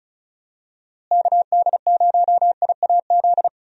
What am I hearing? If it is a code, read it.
Morse code, heard as a keyed tone, about 36 wpm, 700 Hz. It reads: KD0IAZ
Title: USA Call Signs - 6 Characters - Repeated 2x - 35wpm